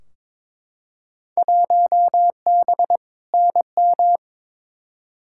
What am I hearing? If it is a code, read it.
Morse code, heard as a keyed tone, about 22 wpm, 700 Hz. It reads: 1B NM
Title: Field Day Exchanges Encore - 22wpm